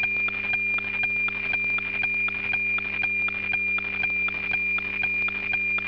Weather fax (radiofax).